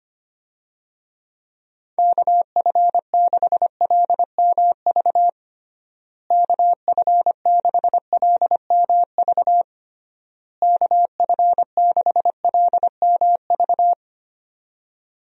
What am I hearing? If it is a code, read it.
Morse code, heard as a keyed tone, about 25 wpm, 700 Hz. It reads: KF6LMV KF6LMV KF6LMV
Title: USA Call Signs - 6 Characters - Repeated 3x - 25wpm